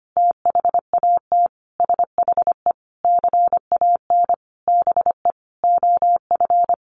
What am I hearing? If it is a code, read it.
Morse code, heard as a keyed tone, about 25 wpm, 700 Hz. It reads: THAT SHE CAN BE OF
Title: Sentences from Top 1200 Words 25wpm